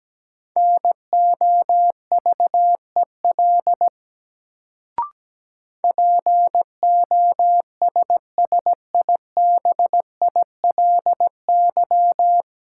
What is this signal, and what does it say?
Morse code, heard as a keyed tone, about 17 wpm, 700 Hz. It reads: NOVEL POSSIBILY
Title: Sentences from Top 1300 Words - New Word Drill - 17wpm